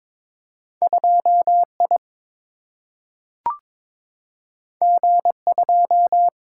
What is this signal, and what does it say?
Morse code, heard as a keyed tone, about 22 wpm, 700 Hz. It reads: 2 I G 2
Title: Single-Letter Single-Number 22wpm